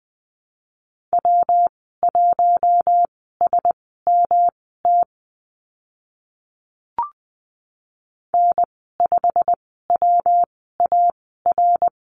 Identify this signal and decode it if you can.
Morse code, heard as a keyed tone, about 20 wpm, spaced as for 15 wpm, 700 Hz. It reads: W1SMT N5WAR
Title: 147 Lesson 24g - Letter y - Call Signs 20wpm